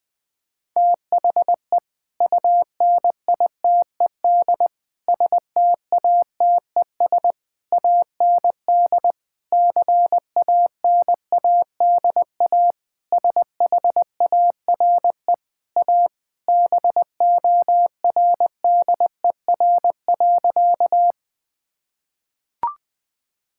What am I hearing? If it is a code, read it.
Morse code, heard as a keyed tone, about 20 wpm, 700 Hz. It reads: THE UNITED STATES AND CANADA SHARE A BORDER.
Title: Sentences with 5-Word Phrases - Spoken Beginning - 20wpm